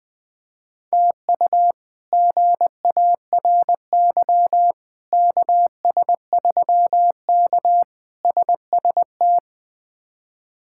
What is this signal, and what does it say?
Morse code, heard as a keyed tone, about 20 wpm, 700 Hz. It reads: TU GARY KS3K SST
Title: K1USN SST QSO Practice - 20wpm